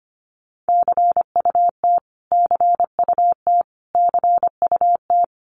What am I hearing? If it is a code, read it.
Morse code, heard as a keyed tone, about 25 wpm, 700 Hz. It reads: CUT CUT CUT
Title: Top 500 Words Repeated 25wpm